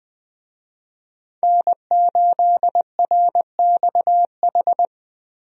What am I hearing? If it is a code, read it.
Morse code, heard as a keyed tone, about 20 wpm, 700 Hz. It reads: N8RXH
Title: USA Call Signs - 5 Characters - Repeated 3x - 20wpm